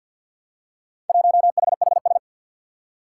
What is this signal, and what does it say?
Morse code, heard as a keyed tone, about 50 wpm, 700 Hz. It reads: 1HHS
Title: Four Challenging Letter-Number Combinations 50wpm